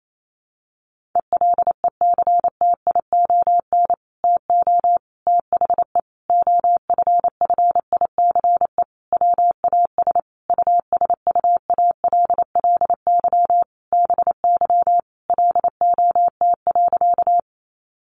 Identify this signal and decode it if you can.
Morse code, heard as a keyed tone, about 28 wpm, 700 Hz. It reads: ELECTION TO THE OFFICE WAS USUALLY BY LOT.
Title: Sentences from Top 1500 Words 28wpm